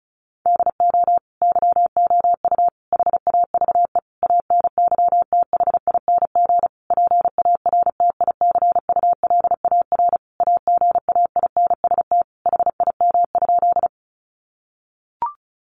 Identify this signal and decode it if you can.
Morse code, heard as a keyed tone, about 34 wpm, 700 Hz. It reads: DO YOU HAVE ANYTHING PARTICULAR AGAINST HIM?
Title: Sentences from Top 1000 Words 35wpm